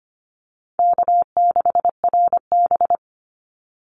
Morse code, heard as a keyed tone, about 25 wpm, 700 Hz. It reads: K6RB
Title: Easy Call Signs Repeated 2x - 25wpm